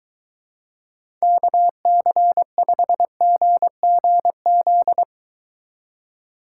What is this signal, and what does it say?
Morse code, heard as a keyed tone, about 23 wpm, 700 Hz. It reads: KC5GGZ